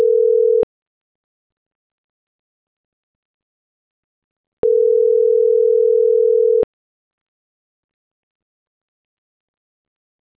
Ringback tone (North America), 440 + 480 Hz.